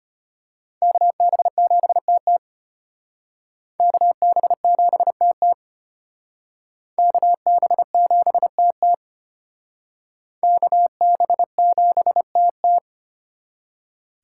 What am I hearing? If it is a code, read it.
Morse code, heard as a keyed tone, about 32 wpm, 700 Hz. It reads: KB7TT KB7TT KB7TT KB7TT